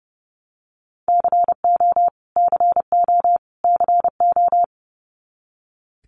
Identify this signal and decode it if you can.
Morse code, heard as a keyed tone, about 30 wpm, 700 Hz. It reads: CO CO CO
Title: US States Repeated 3x 30wpm